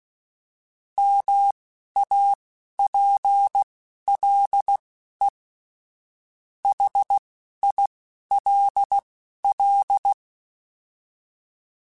Morse code, heard as a keyed tone, about 16 wpm, spaced as for 11 wpm, 800 Hz. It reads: MAPLE HILL